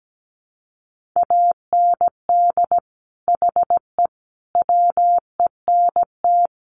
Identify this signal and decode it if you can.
Morse code, heard as a keyed tone, about 17 wpm, 700 Hz. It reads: AND HE WENT
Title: Sentences from Top 900 Words 17wpm